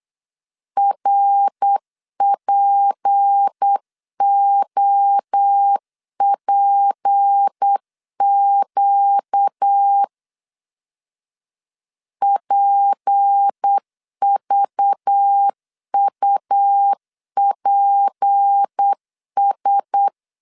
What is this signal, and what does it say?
Morse code, heard as a keyed tone, about 8 wpm, 800 Hz. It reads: RPOPQ PVUPS
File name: LEZIONI / 6 PKX0 / 6 P 8wpm 3025